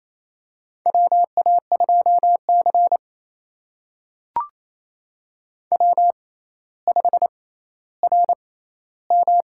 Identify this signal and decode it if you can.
Morse code, heard as a keyed tone, about 27 wpm, spaced as for 14 wpm, 700 Hz. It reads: WA2C W5RM
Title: USA Call Signs - 4 Characters - 6x Character Spacing - 28wpm